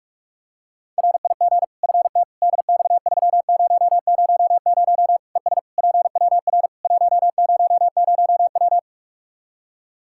Morse code, heard as a keyed tone, about 41 wpm, 700 Hz. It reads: RIG FT DX3000 ES PWR 100W
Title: Practice QSO - Exchange 2 - Info - 45wpm